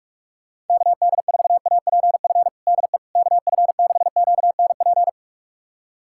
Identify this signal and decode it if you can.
Morse code, heard as a keyed tone, about 44 wpm, 700 Hz. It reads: KD4APF DE KF6QNP
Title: USA Callsign DE Callsign - 6 Characters - 45wpm